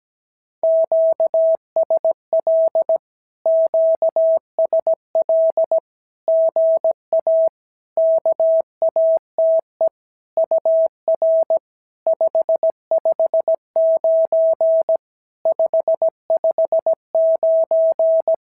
Morse code, heard as a keyed tone, about 16 wpm, 650 Hz. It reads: QSL QSL GA KATE UR 559 559
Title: SOTA QSO Practice 17wpm